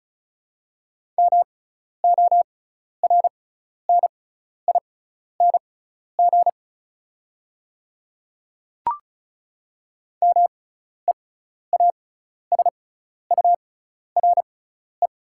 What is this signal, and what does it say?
Morse code, heard as a keyed tone, about 36 wpm, spaced as for 12 wpm, 690 Hz. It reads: MORNING MEASURE
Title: 6x Character Spacing - Top 500 Words - 35wpm